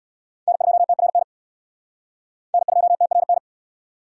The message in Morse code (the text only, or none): N2TRN N2TRN